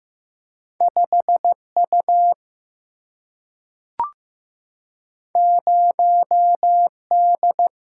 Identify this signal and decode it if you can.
Morse code, heard as a keyed tone, about 15 wpm, 700 Hz. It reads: 5U 0D
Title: Single-Letter Single-Number 15wpm